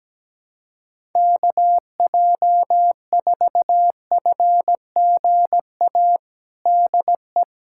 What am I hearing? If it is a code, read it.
Morse code, heard as a keyed tone, about 17 wpm, 700 Hz. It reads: KJ4FGA DE
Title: USA Callsign DE Callsign - 6 Characters - 17wpm